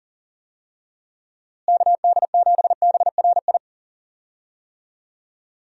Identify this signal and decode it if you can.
Morse code, heard as a keyed tone, about 40 wpm, 700 Hz. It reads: KD7BRI
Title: USA Call Signs - 6 Characters - Repeated 3x - 40wpm